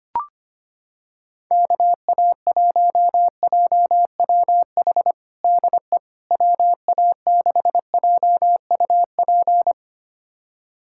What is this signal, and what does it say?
Morse code, heard as a keyed tone, about 25 wpm, 700 Hz. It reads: KA1JWH DE WA6JUP
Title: USA Callsign DE Callsign - 6 Characters - 25wpm